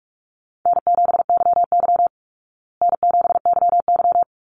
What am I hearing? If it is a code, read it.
Morse code, heard as a keyed tone, about 44 wpm, 700 Hz. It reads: N7YY N7YY
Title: Easy Call Signs Repeated 2x - 45wpm